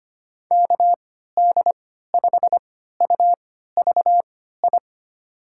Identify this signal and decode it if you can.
Morse code, heard as a keyed tone, about 25 wpm, spaced as for 14 wpm, 700 Hz. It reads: KD5UVI
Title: USA Call Signs - 6 Characters - 3x Character Spacing - 25wpm